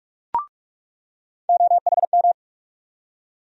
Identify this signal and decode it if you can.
Morse code, heard as a keyed tone, about 47 wpm, 700 Hz. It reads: OHM
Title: USA Call Sign Postfixes - 3 Letters 45wpm